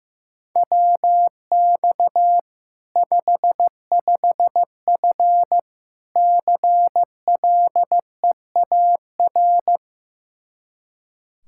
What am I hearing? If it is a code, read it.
Morse code, heard as a keyed tone, about 16 wpm, 700 Hz. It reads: WX 55F CLEAR
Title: Practice QSO - Exchange 2 - Info - 15wpm